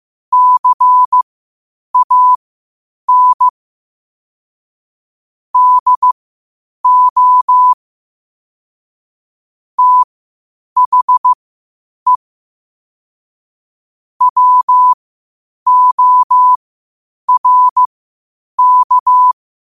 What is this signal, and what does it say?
Morse code, heard as a keyed tone, about 15 wpm, spaced as for 8 wpm, 1000 Hz. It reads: CAN DO THE WORK
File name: qotdfiles / 8wpm